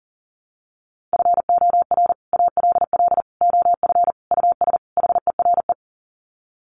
Morse code, heard as a keyed tone, about 40 wpm, 700 Hz. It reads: FOR ALL OF US HERE
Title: Sets of 5 Words 40wpm